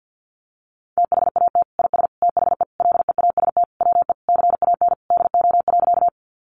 Morse code, heard as a keyed tone, about 50 wpm, 700 Hz. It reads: THAT IS THE LEAST WE CAN DO.
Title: Sentences from Top 700 Words 50wpm